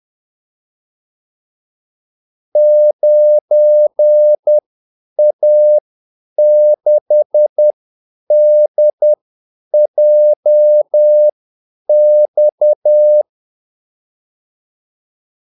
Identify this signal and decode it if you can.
Morse code, heard as a keyed tone, about 10 wpm, spaced as for 8 wpm, 600 Hz. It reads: 9A6DJX
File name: Callsigns-10-8wpm